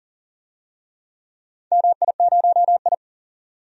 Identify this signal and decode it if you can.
Morse code, heard as a keyed tone, about 40 wpm, 690 Hz. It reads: MI0I
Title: DXCC-100 Call Signs - 4 Characters - 40wpm